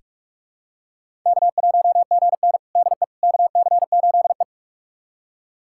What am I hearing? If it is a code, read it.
Morse code, heard as a keyed tone, about 46 wpm, 700 Hz. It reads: K1GN DE KC8E
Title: USA Callsign DE Callsign - 4 Characters - 45wpm